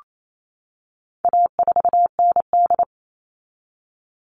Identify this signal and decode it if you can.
Morse code, heard as a keyed tone, about 28 wpm, 700 Hz. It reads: A4ND
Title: All Cut Numbers - 4-Digit 28wpm